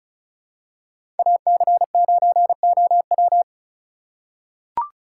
Morse code, heard as a keyed tone, about 34 wpm, 700 Hz. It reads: AC9OW
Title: USA Call Signs - 5 Characters - 4x Character Spacing - 35wpm